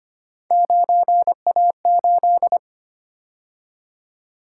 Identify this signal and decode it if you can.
Morse code, heard as a keyed tone, about 29 wpm, 700 Hz. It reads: 9A8
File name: Speed-Racing - Three Letter-Number Combinations - 25wpm